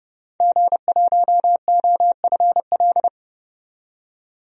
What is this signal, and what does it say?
Morse code, heard as a keyed tone, about 30 wpm, 690 Hz. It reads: G1OFL
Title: DXCC-100 Call Signs - 5 Characters - 30wpm